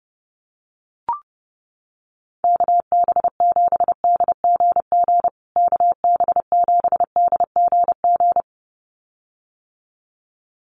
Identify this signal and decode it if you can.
Morse code, heard as a keyed tone, about 30 wpm, 700 Hz. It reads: KB7DGG KB7DGG
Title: USA Call Signs Repeated 2x 30wpm